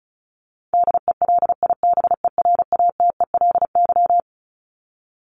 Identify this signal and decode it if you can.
Morse code, heard as a keyed tone, about 34 wpm, 700 Hz. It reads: DELIBERATELY
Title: Top 1751-2000 Words 35wpm